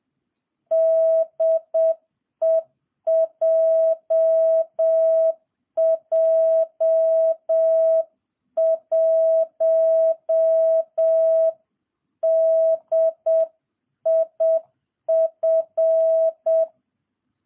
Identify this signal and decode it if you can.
Morse code, heard as a keyed tone, about 7 wpm, 660 Hz. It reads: DE JJ1DIF